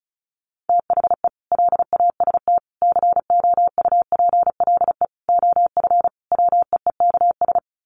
Morse code, heard as a keyed tone, about 34 wpm, 700 Hz. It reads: THE LAST COUPLE OF WEEKS
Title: Sets of 5 Words Encore2 35wpm